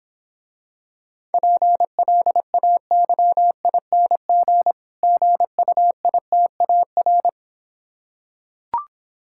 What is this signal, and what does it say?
Morse code, heard as a keyed tone, about 26 wpm, 700 Hz. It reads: PLAYING GUITAR